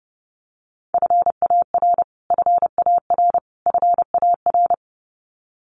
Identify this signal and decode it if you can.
Morse code, heard as a keyed tone, about 30 wpm, 700 Hz. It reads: FAR FAR FAR
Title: Top 500 Words Repeated 30wpm